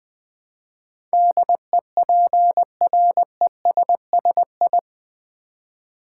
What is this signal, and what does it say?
Morse code, heard as a keyed tone, about 20 wpm, 700 Hz. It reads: DEPRESSI
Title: Build A Word - 4 Letters At A Time - 10-Letter Words - 20wpm